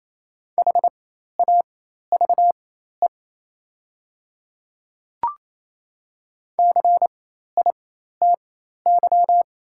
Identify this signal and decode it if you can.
Morse code, heard as a keyed tone, about 28 wpm, spaced as for 13 wpm, 700 Hz. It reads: HAVE CITY
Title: 4x Character Spacing - Top 300 Words - 28wpm